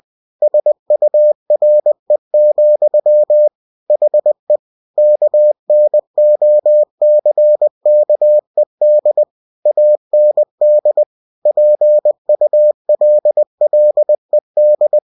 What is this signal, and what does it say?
Morse code, heard as a keyed tone, about 20 wpm, 600 Hz. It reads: SURE, HE KNOCKED AND PULLED
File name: cw-books / winnie-the-pooh / winnie-20wpm-page-0029